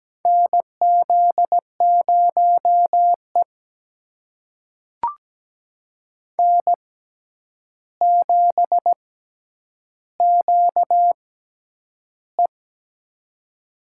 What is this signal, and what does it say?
Morse code, heard as a keyed tone, about 17 wpm, spaced as for 9 wpm, 700 Hz. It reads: NZ0E N7QE
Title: USA Call Signs - 4 Characters - 6x Character Spacing - 17wpm